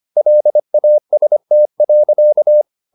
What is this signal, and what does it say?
Morse code, heard as a keyed tone, about 25 wpm, 600 Hz. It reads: LAST.